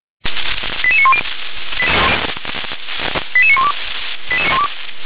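Five-tone selective call (ZVEI), heard as quick runs of five tones, tones of about 70 ms. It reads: 88910 88912
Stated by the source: Received on SDR.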